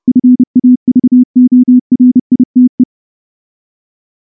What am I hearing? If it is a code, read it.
Morse code, heard as a keyed tone, about 28 wpm, 250 Hz. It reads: FAVORITE
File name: morse-training-30WPM